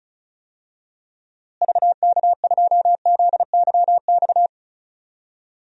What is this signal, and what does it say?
Morse code, heard as a keyed tone, about 36 wpm, 700 Hz. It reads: VK2ZYX
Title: DXCC-100 Call Signs - 6 Characters - Repeated 3x - 35wpm